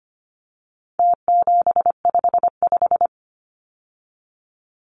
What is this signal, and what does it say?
Morse code, heard as a keyed tone, about 25 wpm, 700 Hz. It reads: T755
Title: Numbers - Contest Style - 4-Digit 25wpm